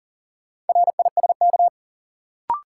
Morse code, heard as a keyed tone, about 40 wpm, 700 Hz. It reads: RISK
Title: Sentences from Top 600 Words - New Word Drill - 40wpm